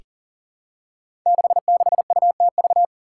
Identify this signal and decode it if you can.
Morse code, heard as a keyed tone, about 38 wpm, 700 Hz. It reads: 66UTV
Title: Five Challenging Letter-Number Combinations 40wpm